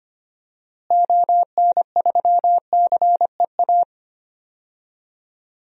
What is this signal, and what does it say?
Morse code, heard as a keyed tone, about 25 wpm, 700 Hz. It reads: ON3CEA
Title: DXCC-100 Call Signs - 6 Characters - Repeated 3x - 25wpm